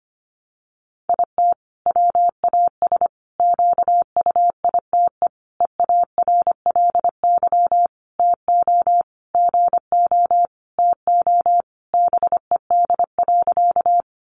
Morse code, heard as a keyed tone, about 25 wpm, 700 Hz. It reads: IT WAS QUITE EARLY TO GO TO BED.
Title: Sentences from Top 700 Words 25wpm